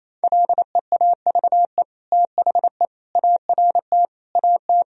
Morse code, heard as a keyed tone, about 28 wpm, 700 Hz. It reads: LEAVE THE ART AT
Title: Sentences from Top 1000 Words - Review Part 3 - 28wpm